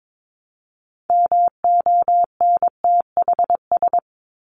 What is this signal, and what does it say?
Morse code, heard as a keyed tone, about 22 wpm, 700 Hz. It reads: MONTHS
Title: ES:S Suffix - Top 100 Words with Base Words 22wpm